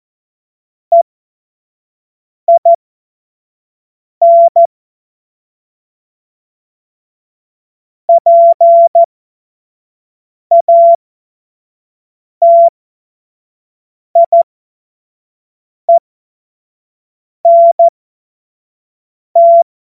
Morse code, heard as a keyed tone, about 14 wpm, spaced as for 5 wpm, 680 Hz. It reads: EIN PATIENT
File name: Morsekurs / Witze / Witz 26 März 2024 25 F70